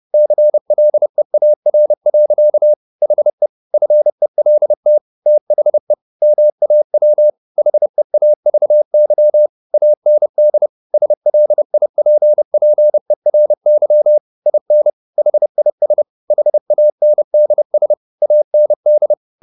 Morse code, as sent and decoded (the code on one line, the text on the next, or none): -.-. .-.. .- .-. .-.-.- / .... / ..-. .-.. - / - .... / -- .- .-- / .... .- ...- -.-- / .- -. -.. / ... .-.. .. .--. .--. .-. -.-- / .. -. / .... .. ... / .... .- -. -.. ... / .- -. -..
CLEAR. HE FELT THE MAW HEAVY AND SLIPPERY IN HIS HANDS AND